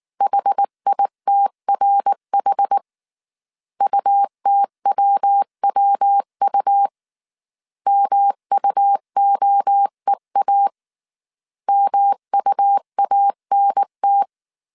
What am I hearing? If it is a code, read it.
Morse code, heard as a keyed tone, about 19 wpm, 800 Hz. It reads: HITRH UTWWU MUOEA MUANT
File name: CW / DET / 1 3 1500C S B 18wpm 1916